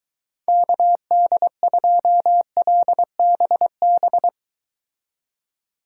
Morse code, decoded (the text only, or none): KD2LBB